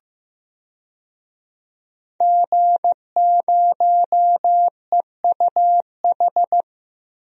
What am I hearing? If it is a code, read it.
Morse code, heard as a keyed tone, about 15 wpm, 700 Hz. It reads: G0EUH
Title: DXCC-100 Call Signs - 5 Characters - 15wpm